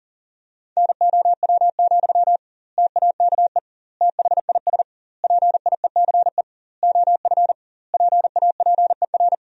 Morse code, heard as a keyed tone, about 40 wpm, 700 Hz. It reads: NOW, TAKE THIS PIECE OF PAPER
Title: 2x Word-Spacing - Sentences from Top 400 Words - 40wpm